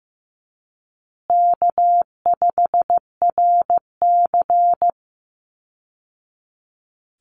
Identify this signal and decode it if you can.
Morse code, heard as a keyed tone, about 15 wpm, 700 Hz. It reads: K5RC
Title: Easy Call Signs Repeated 2x - 15wpm